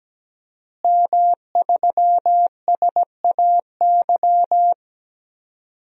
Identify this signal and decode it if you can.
Morse code, heard as a keyed tone, about 17 wpm, 700 Hz. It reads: M3SAY
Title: DXCC-100 Call Signs - 5 Characters - Repeated 3x - 17wpm